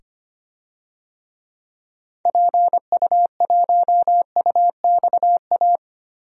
Morse code, heard as a keyed tone, about 25 wpm, 700 Hz. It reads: PU1UXA
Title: DXCC-100 Call Signs - 6 Characters - 25wpm